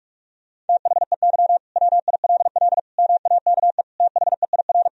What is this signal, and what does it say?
Morse code, heard as a keyed tone, about 43 wpm, 700 Hz. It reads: THEY WILL MAKE THEIR
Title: Sentences from Top 1000 Words - Review Part 3 - 45wpm